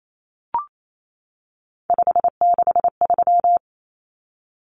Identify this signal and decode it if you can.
Morse code, heard as a keyed tone, about 28 wpm, 700 Hz. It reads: 563
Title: Numbers - 3-Digit 28wpm